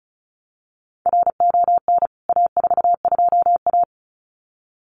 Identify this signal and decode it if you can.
Morse code, heard as a keyed tone, about 34 wpm, 700 Hz. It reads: RON A42A
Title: CWT Exchange 35wpm